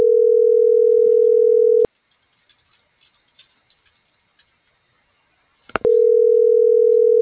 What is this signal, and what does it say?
Ringback tone (North America), 440 + 480 Hz.